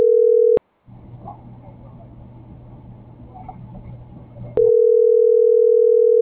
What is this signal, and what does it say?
Ringback tone (North America), 440 + 480 Hz.